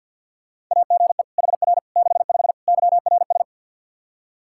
Morse code, heard as a keyed tone, about 49 wpm, 700 Hz. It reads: AGE HR 65 YRS
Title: Practice QSO - Exchange 3 - More Info - 50wpm